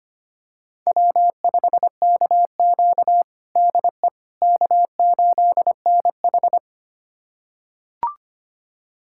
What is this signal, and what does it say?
Morse code, heard as a keyed tone, about 25 wpm, 700 Hz. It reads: W5KQ DE K8NH
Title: USA Callsign DE Callsign - 4 Characters - 25wpm